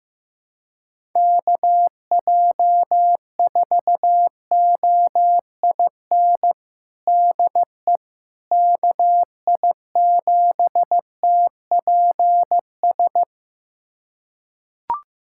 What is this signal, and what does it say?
Morse code, heard as a keyed tone, about 15 wpm, 700 Hz. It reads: KJ4OIN DE KI7TPS
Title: USA Callsign DE Callsign 15wpm